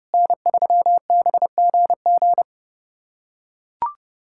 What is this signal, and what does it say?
Morse code, heard as a keyed tone, about 30 wpm, 700 Hz. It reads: N3BGG
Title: USA Callsign DE Callsign 30wpm